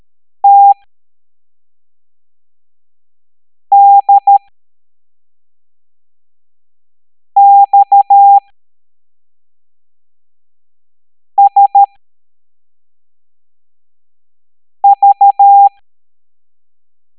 Morse code, heard as a keyed tone, about 13 wpm, spaced as for 3 wpm, 800 Hz. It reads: TDXSV
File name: hamdata / L05-EX01-SVY